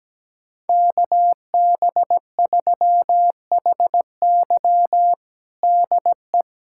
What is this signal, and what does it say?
Morse code, heard as a keyed tone, about 17 wpm, 700 Hz. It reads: KB3HY DE
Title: USA Callsign DE Callsign 17wpm